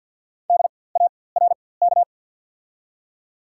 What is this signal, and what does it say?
Morse code, heard as a keyed tone, about 54 wpm, spaced as for 24 wpm, 700 Hz. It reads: DARK
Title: 4x Character Spacing - Top 500 Words - 50wpm